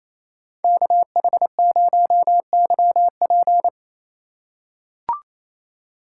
Morse code, heard as a keyed tone, about 28 wpm, 700 Hz. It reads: KH0YP
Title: USA Call Signs - 5 Characters - 2x Character Spacing - 28wpm